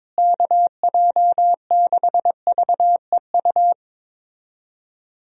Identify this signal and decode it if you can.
Morse code, heard as a keyed tone, about 21 wpm, 700 Hz. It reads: KJ6VEU
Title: USA Call Signs - 6 Characters - Repeated 3x - 22wpm